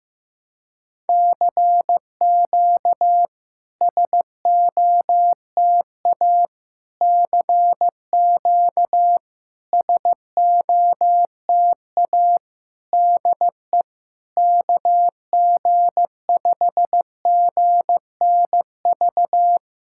Morse code, heard as a keyed tone, about 15 wpm, 700 Hz. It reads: CQ SOTA CQ SOTA DE KG5GNV
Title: SOTA QSO Practice 15wpm